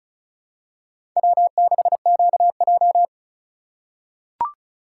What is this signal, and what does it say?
Morse code, heard as a keyed tone, about 36 wpm, 700 Hz. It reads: W6QJ
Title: USA Call Signs - 4x Character Spacing - 35wpm